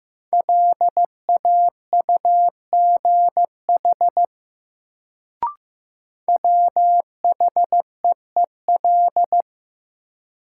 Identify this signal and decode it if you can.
Morse code, heard as a keyed tone, about 15 wpm, 700 Hz. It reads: LAUGH WHEEL
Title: Sentences from Top 500 Words - New Word Drill - 15wpm